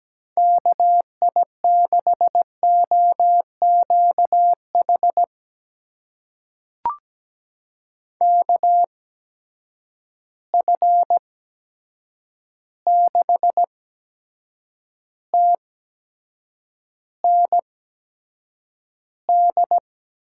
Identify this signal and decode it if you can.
Morse code, heard as a keyed tone, about 17 wpm, spaced as for 7 wpm, 700 Hz. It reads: KI6OQH KF6TND
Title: USA Call Signs - 6 Characters - 8x Character Spacing - 17wpm